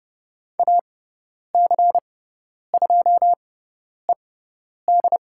Morse code, heard as a keyed tone, about 30 wpm, spaced as for 10 wpm, 700 Hz. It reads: AC2ED
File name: Lesson 17g - Letter u - Call Signs 30wpm